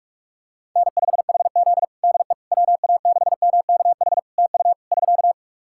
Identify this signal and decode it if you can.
Morse code, heard as a keyed tone, about 41 wpm, 700 Hz. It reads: N5HZ DE WA6MKH TU <SK>
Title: Practice QSO - Exchange 4 - Ending - 45wpm